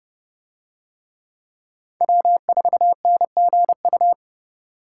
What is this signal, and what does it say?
Morse code, heard as a keyed tone, about 29 wpm, 700 Hz. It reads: W4NGU